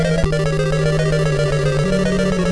There is no code.